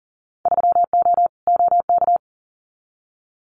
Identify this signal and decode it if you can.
Morse code, heard as a keyed tone, about 40 wpm, 700 Hz. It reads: 3O OK
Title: Winter Field Day Exchanges 40wpm